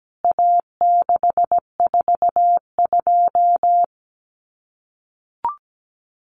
Morse code, heard as a keyed tone, about 17 wpm, 700 Hz. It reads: A642
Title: CWT Exchange 17wpm